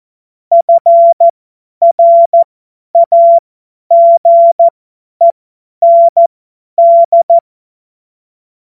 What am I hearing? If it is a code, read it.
Morse code, heard as a keyed tone, about 14 wpm, spaced as for 10 wpm, 680 Hz. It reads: FRAGEND
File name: Morsekurs / Witze / Witz 18. Januar 2022, 50 F70